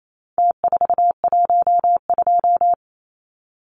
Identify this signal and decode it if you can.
Morse code, heard as a keyed tone, about 28 wpm, 700 Hz. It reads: T412
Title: Numbers - Contest Style - 4-Digit 28wpm